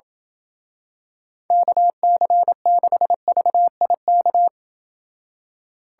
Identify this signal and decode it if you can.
Morse code, heard as a keyed tone, about 28 wpm, 700 Hz. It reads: KC6VIK